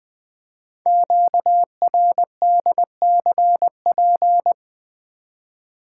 Morse code, heard as a keyed tone, about 20 wpm, 700 Hz. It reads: QRDCP
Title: Code Groups - 5 Letter:Numbers - 20wpm